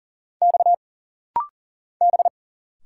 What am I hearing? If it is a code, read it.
Morse code, heard as a keyed tone, about 41 wpm, 700 Hz. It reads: X B
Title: Single Letter-Number - Rapid-Fire 40wpm